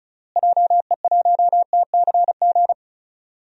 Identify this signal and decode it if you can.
Morse code, heard as a keyed tone, about 36 wpm, 700 Hz. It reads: JE1TCG
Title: DXCC-100 Call Signs - 6 Characters - Repeated 3x - 35wpm